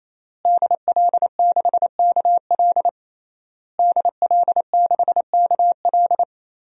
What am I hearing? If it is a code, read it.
Morse code, heard as a keyed tone, about 27 wpm, 700 Hz. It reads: DL6KL DL6KL
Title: DXCC-100 Call Signs - 5 Characters - Repeated 3x - 28wpm